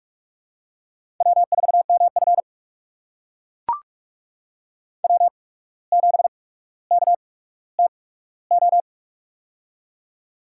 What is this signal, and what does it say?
Morse code, heard as a keyed tone, about 44 wpm, spaced as for 18 wpm, 700 Hz. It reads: W4MF W7KTO
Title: USA Call Signs - 8x Character Spacing - 45wpm